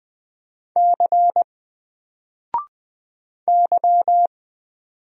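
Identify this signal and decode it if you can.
Morse code, heard as a keyed tone, about 20 wpm, 700 Hz. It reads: C Y
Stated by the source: Single letter and number Morse drill at 20 wpm, from N4LCD.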